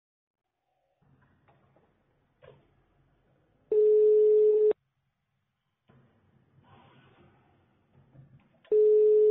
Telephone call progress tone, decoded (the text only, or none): ringback tone